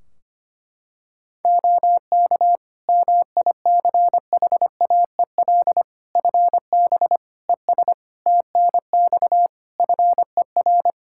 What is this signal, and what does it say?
Morse code, heard as a keyed tone, about 25 wpm, 700 Hz. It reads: OK MICHAEL FB ES TNX FER
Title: K1USN SST QSO Practice 25wpm